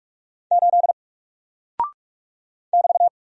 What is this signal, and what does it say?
Morse code, heard as a keyed tone, about 44 wpm, 700 Hz. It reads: 8 <BT>
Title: Single Character (Letter:Number:Punctuation:Prosign) Alt 45wpm